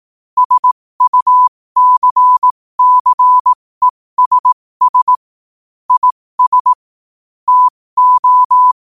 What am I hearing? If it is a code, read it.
Morse code, heard as a keyed tone, about 18 wpm, spaced as for 15 wpm, 1000 Hz. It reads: SUCCESS IS TO
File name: qotdfiles / 15wpm